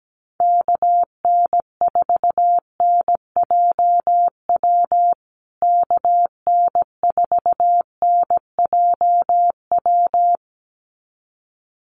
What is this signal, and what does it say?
Morse code, heard as a keyed tone, about 17 wpm, 700 Hz. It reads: KN4NJW KN4NJW
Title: USA Call Signs Repeated 2x 17wpm